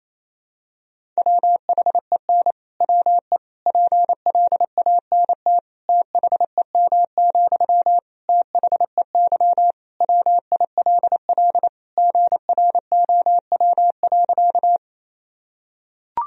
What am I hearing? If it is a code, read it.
Morse code, heard as a keyed tone, about 28 wpm, 700 Hz. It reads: WHEN WE PLANT THEM, THEY WILL GROW.
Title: Sentences from Top 1000 Words - Review Part 1 - 28wpm